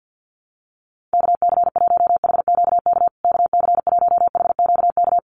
Morse code, heard as a keyed tone, about 48 wpm, 700 Hz. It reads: KC1HQK KC1HQK
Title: USA Call Signs Repeated 3x 50wpm